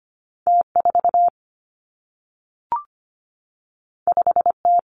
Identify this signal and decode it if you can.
Morse code, heard as a keyed tone, about 25 wpm, 700 Hz. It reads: T4 5T
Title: Two Challenging Letter-Number Combinations 25wpm